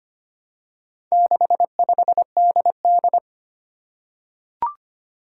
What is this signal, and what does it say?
Morse code, heard as a keyed tone, about 25 wpm, 700 Hz. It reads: 65DD
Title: Four Challenging Letter-Number Combinations 25wpm